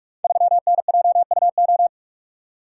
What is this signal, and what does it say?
Morse code, heard as a keyed tone, about 46 wpm, 700 Hz. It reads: 3NJUO
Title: Code Groups - 5 Letter:Numbers - 45wpm